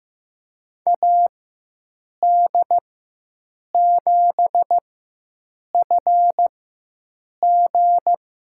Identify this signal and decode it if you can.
Morse code, heard as a keyed tone, about 15 wpm, spaced as for 7 wpm, 700 Hz. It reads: AD7FG
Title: USA Call Signs - 4x Character Spacing - 15wpm